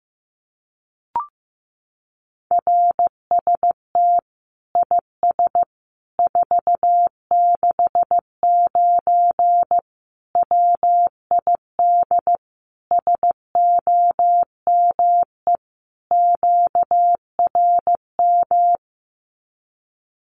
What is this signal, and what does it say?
Morse code, heard as a keyed tone, about 15 wpm, 700 Hz. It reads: RST IS 469 WID SOME QRM
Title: Common QSO Phrases 15wpm